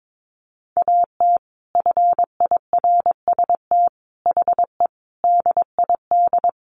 Morse code, heard as a keyed tone, about 22 wpm, 700 Hz. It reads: AT FIRST HE DID
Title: Sentences from Top 600 Words 22wpm